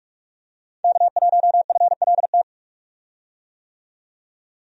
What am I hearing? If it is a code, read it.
Morse code, heard as a keyed tone, about 43 wpm, 700 Hz. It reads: K1FLT
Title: USA Call Signs - 5 Characters - 45wpm